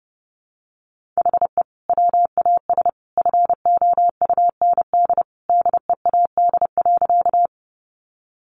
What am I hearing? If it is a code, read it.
Morse code, heard as a keyed tone, about 30 wpm, 700 Hz. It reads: HE WAS FOUND DEAD.
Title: Sentences from Top 1500 Words 30wpm